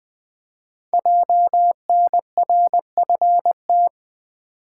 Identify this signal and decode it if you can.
Morse code, heard as a keyed tone, about 20 wpm, 700 Hz. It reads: JNRFT
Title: Code Groups - 5 Letter:Numbers - 20wpm